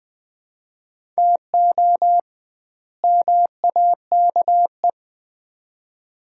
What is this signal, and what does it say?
Morse code, heard as a keyed tone, about 20 wpm, 700 Hz. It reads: TO MAKE
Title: 2x Word-Spacing - Sets of 2 Words - 20wpm